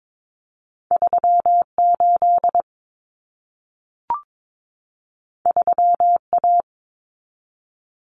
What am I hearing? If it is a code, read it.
Morse code, heard as a keyed tone, about 22 wpm, 700 Hz. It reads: 38 3A
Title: Common Cut Numbers - 2-Digit 22wpm